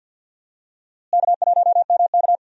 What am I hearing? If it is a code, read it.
Morse code, heard as a keyed tone, about 46 wpm, 700 Hz. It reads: K1MK
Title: USA Call Signs - 4 Characters - Repeated 3x - 50wpm